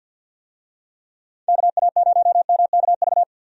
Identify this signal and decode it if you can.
Morse code, heard as a keyed tone, about 51 wpm, 700 Hz. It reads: KA0MKV
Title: USA Call Signs - 6 Characters - 50wpm